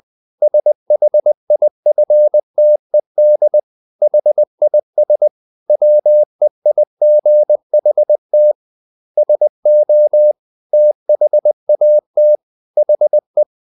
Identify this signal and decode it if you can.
Morse code, heard as a keyed tone, about 20 wpm, 600 Hz. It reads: SHIFTED HIS WEIGHT SO THAT HE
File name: the old man and the sea part 2 20wpm